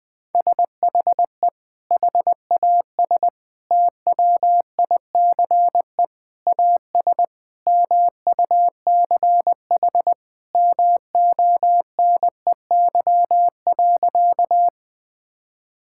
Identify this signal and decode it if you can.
Morse code, heard as a keyed tone, about 21 wpm, 700 Hz. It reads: SHE HAS TWICE AS MUCH MONEY.
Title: Sentences from Top 1600 Words 20wpm